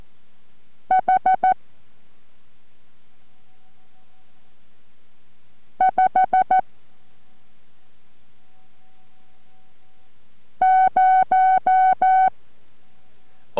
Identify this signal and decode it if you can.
Morse code, heard as a keyed tone, about 14 wpm, 750 Hz. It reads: H 5 0